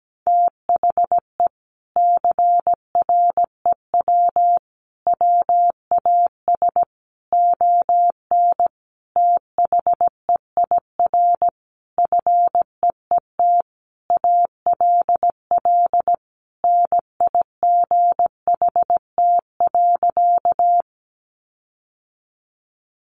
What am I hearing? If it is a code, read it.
Morse code, heard as a keyed tone, about 17 wpm, 700 Hz. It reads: THE CREW WAS ON THEIR FEET ALL NIGHT.
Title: Sentences from Top 1500 Words 17wpm